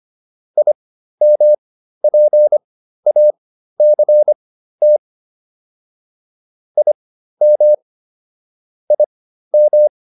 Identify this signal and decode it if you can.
Morse code, heard as a keyed tone, about 25 wpm, spaced as for 13 wpm, 600 Hz. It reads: IMPACT IM IM